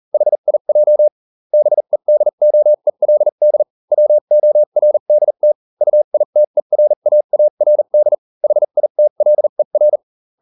Morse code, heard as a keyed tone, about 40 wpm, spaced as for 35 wpm, 600 Hz. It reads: HIJ BEDOELD WORDT UITERAARD HITLER